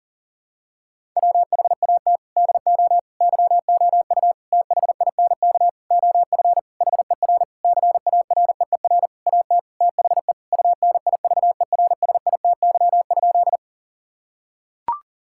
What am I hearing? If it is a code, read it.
Morse code, heard as a keyed tone, about 40 wpm, 700 Hz. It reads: WHAT DO YOU THINK OF HER CAREER AT THE UNIVERSITY?
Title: Sentences from Top 1000 Words - Review Part 6 - 40wpm